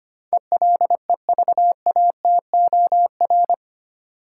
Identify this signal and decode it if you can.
Morse code, heard as a keyed tone, about 25 wpm, 700 Hz. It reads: ELEVATOR
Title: Sentences from Top 1600 Words - New Word Drill - 25wpm